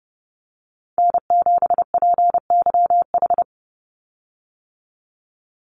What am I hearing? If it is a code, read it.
Morse code, heard as a keyed tone, about 30 wpm, 700 Hz. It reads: N7PYH
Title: USA Call Signs 30wpm